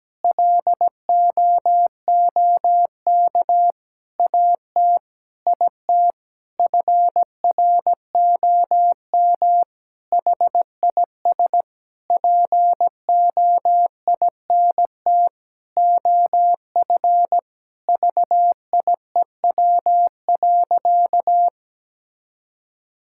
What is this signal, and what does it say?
Morse code, heard as a keyed tone, about 17 wpm, 700 Hz. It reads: LOOK AT IT FROM HIS POINT OF VIEW.
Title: Sentences from Top 1000 Words - Review Part 6 - 17wpm